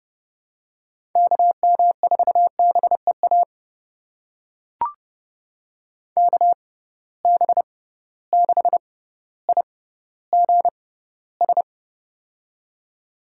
Morse code, heard as a keyed tone, about 30 wpm, spaced as for 15 wpm, 700 Hz. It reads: KM4BEA KB6IGS
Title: USA Call Signs - 6 Characters - 6x Character Spacing - 30wpm